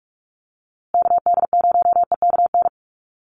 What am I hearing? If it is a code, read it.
Morse code, heard as a keyed tone, about 44 wpm, 700 Hz. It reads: KD0EKN
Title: USA Call Signs Repeated 3x 45wpm